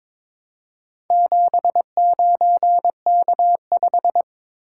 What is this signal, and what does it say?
Morse code, heard as a keyed tone, about 22 wpm, 700 Hz. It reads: 79K5
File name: practice / 22wpm / Four Letter-Number Combinations - 22wpm